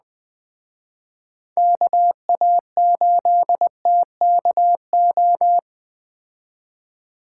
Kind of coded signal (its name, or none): Morse code